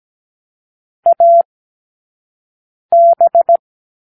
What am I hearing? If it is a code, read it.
Morse code, heard as a keyed tone, about 17 wpm, spaced as for 5 wpm, 680 Hz. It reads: AB